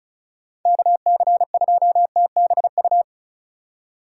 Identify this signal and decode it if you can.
Morse code, heard as a keyed tone, about 36 wpm, 700 Hz. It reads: KC2TBU
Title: USA Call Signs - 6 Characters - 2x Character Spacing - 35wpm